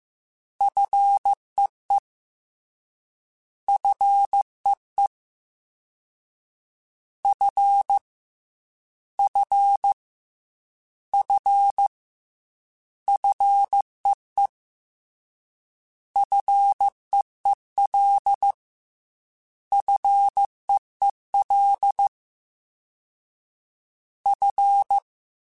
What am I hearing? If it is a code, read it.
Morse code, heard as a keyed tone, about 15 wpm, 800 Hz. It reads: FEE FEE F F F FEE FEEL FEEL F
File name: F and L / F and L mp3 / F and L Prefixes 3x2 Twice @13